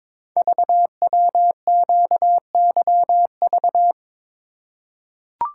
Morse code, heard as a keyed tone, about 22 wpm, 700 Hz. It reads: VWQYV
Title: Code Groups - 5 Letters - 22wpm